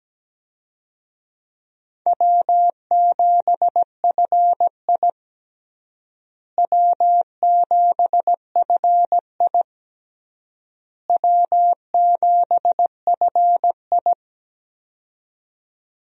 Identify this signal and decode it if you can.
Morse code, heard as a keyed tone, about 17 wpm, 700 Hz. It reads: W7FI W7FI W7FI
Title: USA Call Signs - 4 Characters - Repeated 3x - 17wpm